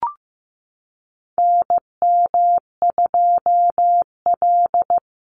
Morse code, heard as a keyed tone, about 15 wpm, 700 Hz. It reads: NM2L
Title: Call Signs Repeated 3x - 15wpm